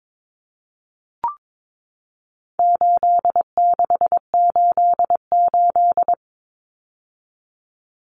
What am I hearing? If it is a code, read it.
Morse code, heard as a keyed tone, about 22 wpm, 700 Hz. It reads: 8688
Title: Numbers - 4-Digit 22wpm